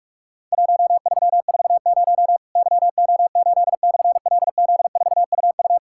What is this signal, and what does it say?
Morse code, heard as a keyed tone, about 43 wpm, 700 Hz. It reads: 1340 YO8/LZ4UU
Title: Worked All Europe DX Contest - QTC Exchange - 45wpm